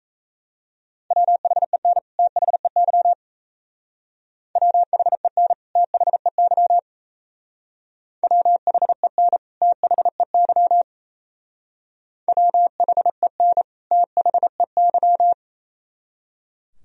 Morse code, heard as a keyed tone, about 33 wpm, 690 Hz. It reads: WHEN THEY WHEN THEY WHEN THEY WHEN THEY
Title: Speed-Racing - Sets of 2 Words 28wpm